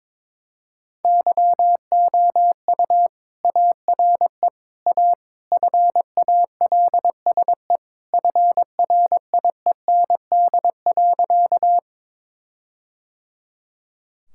Morse code, heard as a keyed tone, about 22 wpm, 700 Hz. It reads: YOU ARE A FALSE FRIEND.
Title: Sentences from Top 1600 Words 22wpm